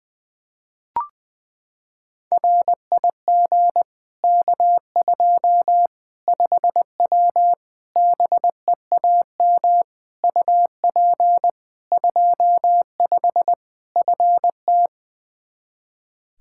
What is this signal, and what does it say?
Morse code, heard as a keyed tone, about 20 wpm, 700 Hz. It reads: RIG K2 5W BEAM UP 25 FT
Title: Practice QSO - Exchange 2 - Info - 20wpm